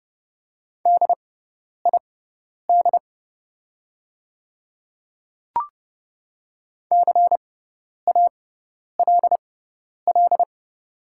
Morse code, heard as a keyed tone, about 30 wpm, spaced as for 10 wpm, 700 Hz. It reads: DID CALL
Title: 6x Character Spacing - Top 100 Words - 30wpm